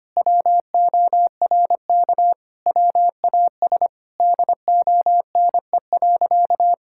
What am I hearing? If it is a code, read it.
Morse code, heard as a keyed tone, about 25 wpm, 700 Hz. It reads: WORK WAS DONE.
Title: Sentences from Top 1000 Words - Review Part 5 - 25wpm